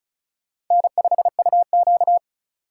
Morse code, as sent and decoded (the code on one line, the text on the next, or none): -. ..... ..- --.-
N5UQ